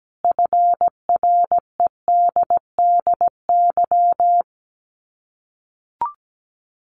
Morse code, heard as a keyed tone, about 17 wpm, 700 Hz. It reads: FREDDY
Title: Common QSO Phrases 17wpm